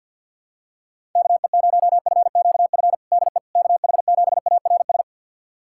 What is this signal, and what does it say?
Morse code, heard as a keyed tone, about 50 wpm, 700 Hz. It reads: KE0FQF DE KH7ARS
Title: USA Callsign DE Callsign - 6 Characters - 50wpm